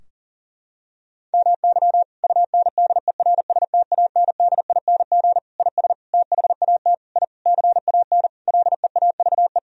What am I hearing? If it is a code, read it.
Morse code, heard as a keyed tone, about 40 wpm, 700 Hz. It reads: MY UNDERSTANDING IS THAT I CAN LEAVE
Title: Sentences from Top 1000 Words - Review Part 3 - 40wpm